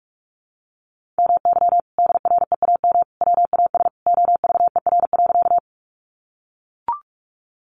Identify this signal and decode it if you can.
Morse code, heard as a keyed tone, about 44 wpm, 700 Hz. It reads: MY DREAM WAS OVER.
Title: Sentences from Top 1500 Words 45wpm